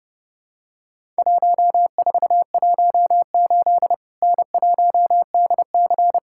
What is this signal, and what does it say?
Morse code, heard as a keyed tone, about 30 wpm, 700 Hz. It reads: 1418 N1DC
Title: Worked All Europe DX Contest - QTC Exchange - 30wpm